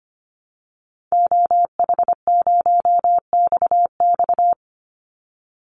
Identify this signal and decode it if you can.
Morse code, heard as a keyed tone, about 25 wpm, 700 Hz. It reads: OH0XX
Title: Call Signs Repeated 3x - 25wpm